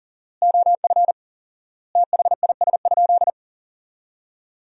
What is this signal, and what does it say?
Morse code, heard as a keyed tone, about 40 wpm, 700 Hz. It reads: OF THIS?
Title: 4x Word-Spacing - Sentences from Top 200 Words - 40wpm